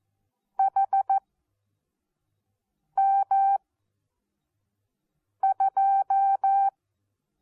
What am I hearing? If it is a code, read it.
Morse code, heard as a keyed tone, about 14 wpm, spaced as for 4 wpm, 780 Hz. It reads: HM2
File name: Onlinelehrgang / morsen / DJ4UF Morsen14